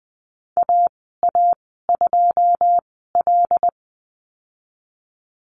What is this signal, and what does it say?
Morse code, heard as a keyed tone, about 20 wpm, spaced as for 14 wpm, 700 Hz. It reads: AA2L
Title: 179 Lesson 29g - Slash - Call Signs 20wpm